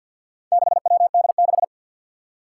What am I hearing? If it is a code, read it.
Morse code, heard as a keyed tone, about 51 wpm, 700 Hz. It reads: 6WD6
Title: Four Challenging Letter-Number Combinations 50wpm